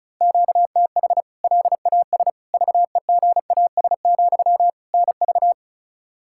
Morse code, heard as a keyed tone, about 36 wpm, 700 Hz. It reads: QTH LAS VEGAS, NV
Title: K1USN SST QSO Practice 35wpm